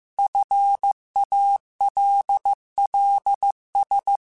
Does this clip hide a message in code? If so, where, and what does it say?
Yes: Morse code from 0.2 to 4.2 s, heard as a keyed tone, about 15 wpm, 800 Hz. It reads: FALLS